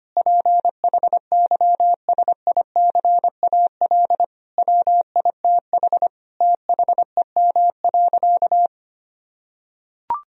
Morse code, heard as a keyed tone, about 25 wpm, 700 Hz. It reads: PHYSICAL WITH THEM.
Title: Sentences from Top 1000 Words - Review Part 4 - 25wpm